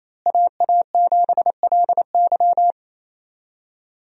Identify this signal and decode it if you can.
Morse code, heard as a keyed tone, about 28 wpm, 700 Hz. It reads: AA7LY
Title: USA Call Signs - 5 Characters - Repeated 3x - 28wpm